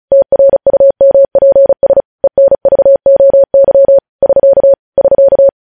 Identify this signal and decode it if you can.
Morse code, heard as a keyed tone, about 36 wpm, 550 Hz. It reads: TRUMPS ENVOY <SK> <SK>
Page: News headlines in Morse code at 35 wpm